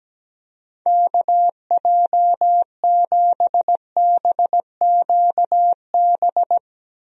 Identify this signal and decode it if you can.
Morse code, heard as a keyed tone, about 17 wpm, 700 Hz. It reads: KJ7BQB
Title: USA Call Signs - 6 Characters - Repeated 3x - 17wpm